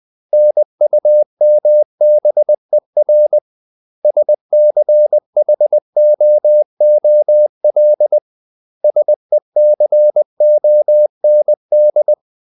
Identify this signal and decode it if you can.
Morse code, heard as a keyed tone, about 20 wpm, 600 Hz. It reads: NUMBER SCHOOL SECOND
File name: CW / Practice / 500MostCommonEnglishWords20